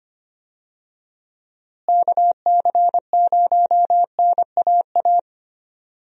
Morse code, heard as a keyed tone, about 25 wpm, 700 Hz. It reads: KC0NAA
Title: USA Call Signs - 6 Characters - Repeated 3x - 25wpm